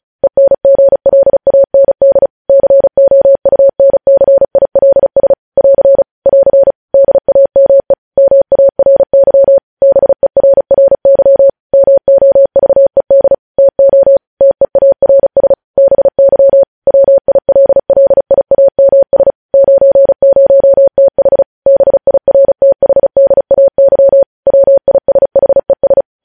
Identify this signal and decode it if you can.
Morse code, heard as a keyed tone, about 36 wpm, 550 Hz. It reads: ENGLAND COUNCILS <AR> <AR> DAME MARY BERRY MOVED TO TEARS BY WILLIAMS 90TH BIRTHDAY WISHES
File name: podcast / News headlines-1x-35x35